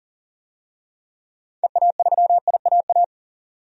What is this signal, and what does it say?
Morse code, heard as a keyed tone, about 40 wpm, 700 Hz. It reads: EA3IAA
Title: DXCC-100 Call Signs - 6 Characters - 40wpm